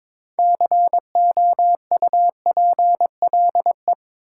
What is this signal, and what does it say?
Morse code, heard as a keyed tone, about 21 wpm, 700 Hz. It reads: COUPLE
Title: Sets of 2 Words - New Word Drill - 22wpm